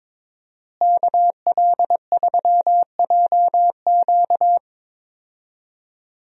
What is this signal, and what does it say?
Morse code, heard as a keyed tone, about 22 wpm, 700 Hz. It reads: KL3JQ
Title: USA Call Signs - 5 Characters - Repeated 2x - 22wpm